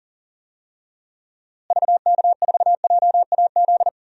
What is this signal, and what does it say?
Morse code, heard as a keyed tone, about 40 wpm, 700 Hz. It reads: VK4JAZ
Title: DXCC-100 Call Signs - 6 Characters - Repeated 3x - 40wpm